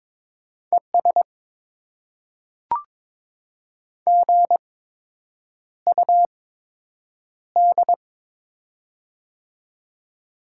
Morse code, heard as a keyed tone, about 22 wpm, spaced as for 8 wpm, 700 Hz. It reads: ES GUD
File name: Top 20 QSO Elements - 8x Character Spacing - 22wpm